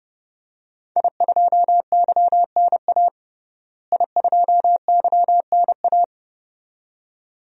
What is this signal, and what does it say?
Morse code, heard as a keyed tone, about 32 wpm, 700 Hz. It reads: I2YNA I2YNA
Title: DXCC-100 Call Signs - 5 Characters - Repeated 2x - 30wpm